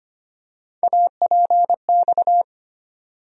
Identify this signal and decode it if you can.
Morse code, heard as a keyed tone, about 25 wpm, 700 Hz. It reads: APX
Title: USA Call Sign Postfixes - 3 Letters 25wpm